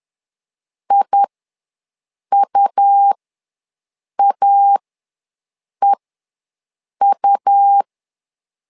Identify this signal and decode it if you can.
Morse code, heard as a keyed tone, about 11 wpm, spaced as for 6 wpm, 800 Hz. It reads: IUAEU
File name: LEZIONI / 2 UHDN / 2 U ws 10wpm 2017